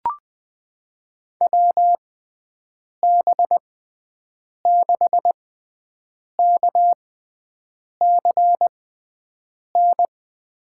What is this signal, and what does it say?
Morse code, heard as a keyed tone, about 20 wpm, spaced as for 7 wpm, 700 Hz. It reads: WB6KCN
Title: USA Call Signs - 6 Characters - 6x Character Spacing - 20wpm